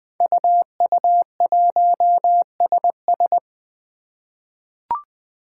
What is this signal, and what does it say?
Morse code, heard as a keyed tone, about 20 wpm, 700 Hz. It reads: UU1SS
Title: Five Challenging Letter-Number Combinations 20wpm